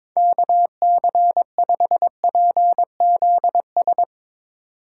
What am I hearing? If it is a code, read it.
Morse code, heard as a keyed tone, about 22 wpm, 700 Hz. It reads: KC5PZS
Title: Speed Racing - USA 2x3 Call Signs 22wpm